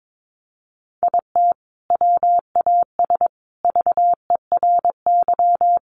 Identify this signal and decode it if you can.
Morse code, heard as a keyed tone, about 22 wpm, 700 Hz. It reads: IT WAS VERY
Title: Sentences from Top 1100 Words 22wpm